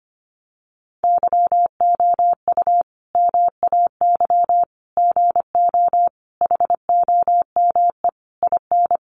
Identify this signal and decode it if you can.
Morse code, heard as a keyed tone, about 25 wpm, 700 Hz. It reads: YOU MAY GO HOME IN
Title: Sentences from Top 1500 Words 25wpm